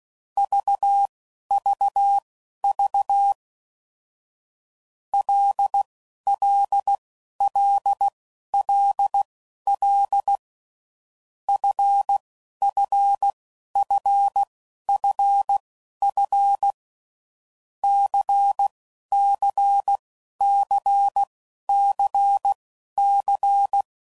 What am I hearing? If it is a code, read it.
Morse code, heard as a keyed tone, about 16 wpm, spaced as for 11 wpm, 800 Hz. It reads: VVV LLLLL FFFFF CCCCC
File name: Groups / Basic Letter Groups / LFCYQ